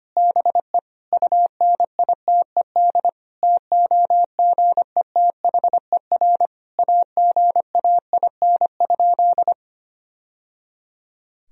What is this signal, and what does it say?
Morse code, heard as a keyed tone, about 25 wpm, 700 Hz. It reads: BE UNITED TOGETHER AGAIN?
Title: Sentences from Top 1000 Words - Review Part 4 - 25wpm